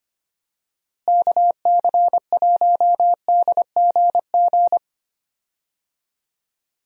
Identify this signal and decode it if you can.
Morse code, heard as a keyed tone, about 25 wpm, 700 Hz. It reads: KC1DGG
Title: USA Call Signs - 6 Characters - Repeated 2x - 25wpm